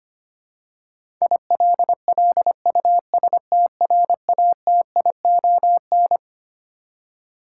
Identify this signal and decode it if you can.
Morse code, heard as a keyed tone, about 25 wpm, 700 Hz. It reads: ILLUSTRATION
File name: Twelve Letter Words - 25wpm